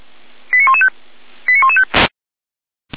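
Five-tone selective call (ZVEI), heard as quick runs of five tones, tones of about 70 ms. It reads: 89115 89115
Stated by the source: ZVEI five-tone call 89358 received on an SDR, fire brigade Feuerwehr Orlamünde.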